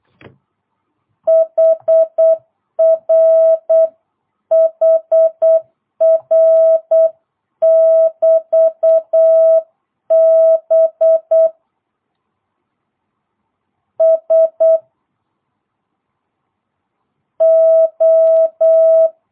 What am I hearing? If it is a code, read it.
Morse code, heard as a keyed tone, about 8 wpm, 650 Hz. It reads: HR HR <BT> B S O